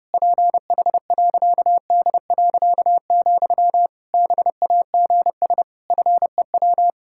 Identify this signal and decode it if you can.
Morse code, heard as a keyed tone, about 30 wpm, 700 Hz. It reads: PH.D., BAGS FEW
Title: Pangram Challenge v2 - 30wpm